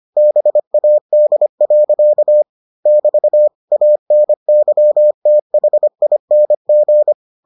Morse code, heard as a keyed tone, about 25 wpm, 600 Hz. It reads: BAD. <BT> ANYTHING